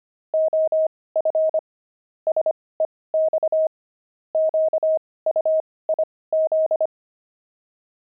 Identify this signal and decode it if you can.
Morse code, heard as a keyed tone, about 25 wpm, spaced as for 18 wpm, 630 Hz. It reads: OF SEX QUIZ
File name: pangram-fs-18-cs-25-wived